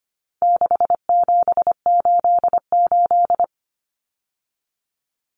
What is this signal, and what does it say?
Morse code, heard as a keyed tone, about 25 wpm, 700 Hz. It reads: 6788
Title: Numbers - 4-Digit 25wpm